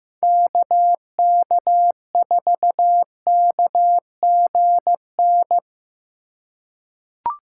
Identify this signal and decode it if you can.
Morse code, heard as a keyed tone, about 15 wpm, 700 Hz. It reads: KK4KGN
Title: USA Call Signs - 6 Characters - 4x Character Spacing - 15wpm